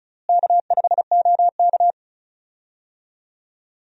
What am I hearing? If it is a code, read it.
Morse code, heard as a keyed tone, about 34 wpm, 700 Hz. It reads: K5OK
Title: USA Call Signs - 4 Characters - Repeated 2x - 35wpm